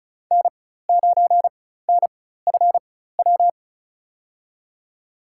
Morse code, heard as a keyed tone, about 36 wpm, spaced as for 16 wpm, 700 Hz. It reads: N9NFW
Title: USA Call Signs - 4x Character Spacing - 35wpm